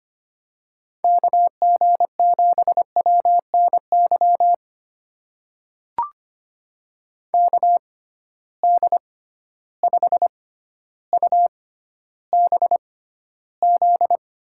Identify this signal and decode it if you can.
Morse code, heard as a keyed tone, about 25 wpm, spaced as for 13 wpm, 700 Hz. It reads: KG7WNY KD5UBZ
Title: USA Call Signs - 6x Character Spacing - 25wpm